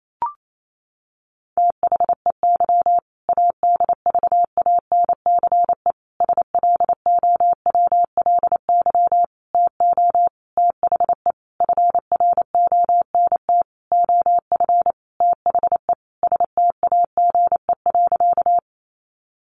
Morse code, heard as a keyed tone, about 28 wpm, 700 Hz. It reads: THEY ADVANCE SLOWLY TO THE FRONT OF THE STAGE.
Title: Sentences from Top 1400 Words 28wpm